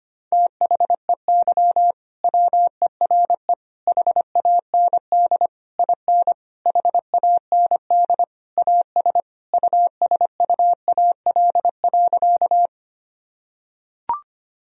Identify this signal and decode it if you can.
Morse code, heard as a keyed tone, about 25 wpm, 700 Hz. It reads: THEY WERE HAND IN HAND AS USUAL.
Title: Sentences from Top 1000 Words - Review Part 2 - 25wpm